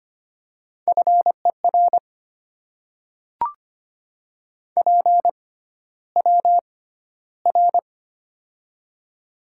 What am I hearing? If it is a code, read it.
Morse code, heard as a keyed tone, about 25 wpm, spaced as for 13 wpm, 700 Hz. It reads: FER PWR
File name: Top 90 QSO Elements - 6x Character Spacing - 25wpm